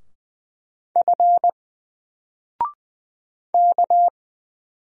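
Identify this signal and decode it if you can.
Morse code, heard as a keyed tone, about 20 wpm, 700 Hz. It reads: F K
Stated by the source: Single letter and number Morse drill at 20 wpm, from N4LCD.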